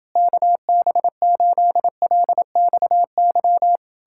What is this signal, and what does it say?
Morse code, heard as a keyed tone, about 30 wpm, 700 Hz. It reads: KB8LXY